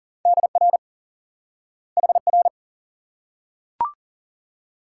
Morse code, heard as a keyed tone, about 40 wpm, 700 Hz. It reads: DR HR
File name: IQE Course 3 - Lesson 89 - YAGI - Two Elements - 40wpm